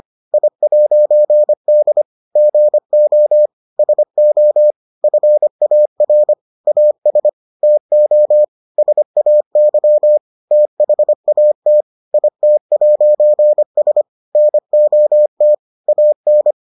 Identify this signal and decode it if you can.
Morse code, heard as a keyed tone, about 25 wpm, 600 Hz. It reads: I'D GO SO FAR AS TO SAY THAT IT'S NOT AN